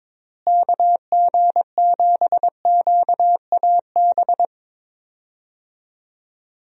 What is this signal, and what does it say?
Morse code, heard as a keyed tone, about 22 wpm, 700 Hz. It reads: KG7QAB
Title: USA Call Signs - 6 Characters - Repeated 3x - 22wpm